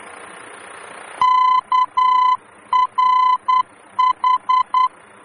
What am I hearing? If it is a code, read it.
Morse code, heard as a keyed tone, about 10 wpm, 1010 Hz. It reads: KRH
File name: germany / mx / KRH-D-VOR